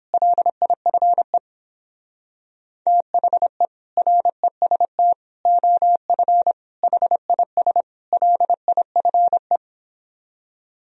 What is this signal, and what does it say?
Morse code, heard as a keyed tone, about 29 wpm, 700 Hz. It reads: LIFE THE REST OF HIS LIFE